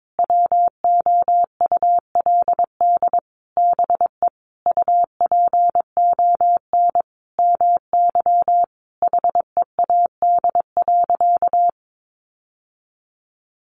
Morse code, heard as a keyed tone, about 22 wpm, 700 Hz. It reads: WOULD BE UPON MY HEAD.
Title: Sentences from Top 1100 Words 22wpm